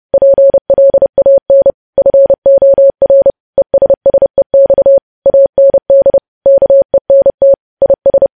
Morse code, heard as a keyed tone, about 30 wpm, 550 Hz. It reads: PLAN FOR ESSEX AND KENT IS
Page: news headlines in morse code at 30 wpm